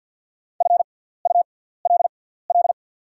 Morse code, heard as a keyed tone, about 48 wpm, spaced as for 17 wpm, 700 Hz. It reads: FULL
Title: 6x Character Spacing - Top 500 Words - 50wpm